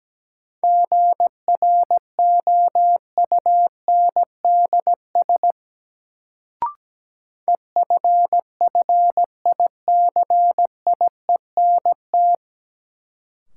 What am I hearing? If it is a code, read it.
Morse code, heard as a keyed tone, about 17 wpm, 700 Hz. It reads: GROUNDS EFFICIENT
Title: Sentences from Top 1300 Words - New Word Drill - 17wpm